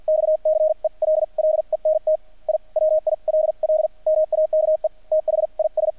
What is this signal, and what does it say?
Morse code, heard as a keyed tone, about 50 wpm, spaced as for 42 wpm, 620 Hz, with radio noise. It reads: XCELLENT I WILL MAKE THIS
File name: archived / CRSnet